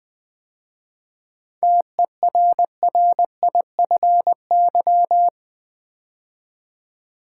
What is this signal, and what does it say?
Morse code, heard as a keyed tone, about 20 wpm, 700 Hz. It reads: TERRIFY
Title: 7 Letter Words - First 3 Letters Given - 20wpm